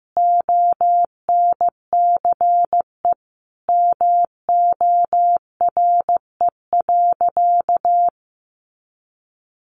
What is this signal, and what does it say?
Morse code, heard as a keyed tone, about 15 wpm, 700 Hz. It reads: ONCE MORE.
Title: Sentences from Top 500 Words 15wpm